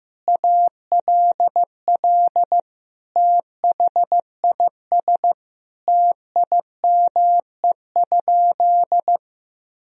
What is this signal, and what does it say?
Morse code, heard as a keyed tone, about 16 wpm, 700 Hz. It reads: ALL THIS TIME?
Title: Sentences from Top 1000 Words - Review Part 2 - 15wpm